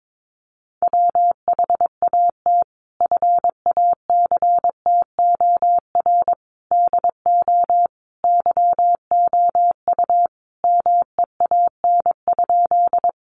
Morse code, heard as a keyed tone, about 22 wpm, 700 Hz. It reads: WHAT FACTOR DO YOU MEAN?
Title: Sentences from Top 1300 Words 22wpm